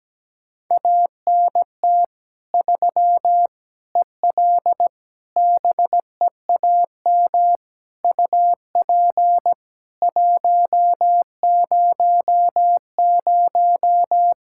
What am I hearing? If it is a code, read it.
Morse code, heard as a keyed tone, about 17 wpm, 700 Hz. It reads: ANT 3 EL BEAM UP 100
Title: K1USN SST QSO Practice 17wpm